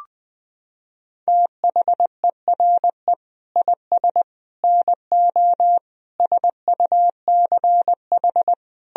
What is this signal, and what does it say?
Morse code, heard as a keyed tone, about 21 wpm, 700 Hz. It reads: THERE IS NO SUCH
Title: Sentences from Top 1000 Words - Review Part 1 - 20wpm